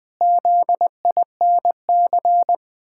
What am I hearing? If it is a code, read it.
Morse code, heard as a keyed tone, about 20 wpm, 700 Hz. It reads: ZINC
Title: Pangram Challenge v2 - 20wpm